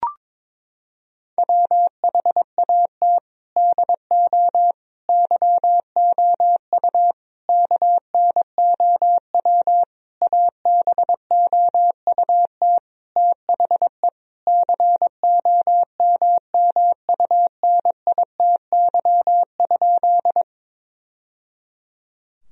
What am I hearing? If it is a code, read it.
Morse code, heard as a keyed tone, about 22 wpm, 700 Hz. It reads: WHAT DO YOU KNOW ABOUT THE COMMUNITY?
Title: Sentences from Top 1000 Words - Review Part 4 - 22wpm